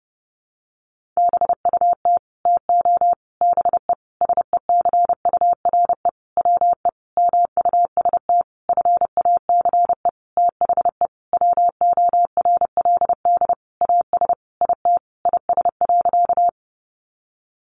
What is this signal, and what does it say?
Morse code, heard as a keyed tone, about 30 wpm, 700 Hz. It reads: BUT TO BE SECURE WE MUST FACE THE WORLD AS IT IS.
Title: Sentences from Top 1200 Words 30wpm